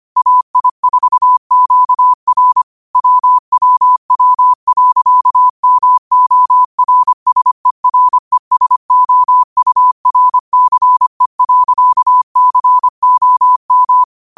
Morse code, heard as a keyed tone, about 25 wpm, 1000 Hz. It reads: AI4QR WWW.MORSERESOURCE.COM
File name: qotdfiles / 25wpm